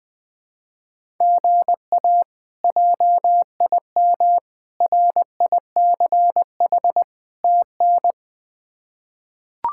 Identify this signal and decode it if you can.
Morse code, heard as a keyed tone, about 20 wpm, 700 Hz. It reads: GA JIM RICH TN
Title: K1USN SST QSO Practice - 20wpm